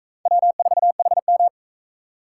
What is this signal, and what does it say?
Morse code, heard as a keyed tone, about 40 wpm, 700 Hz. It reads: W4HM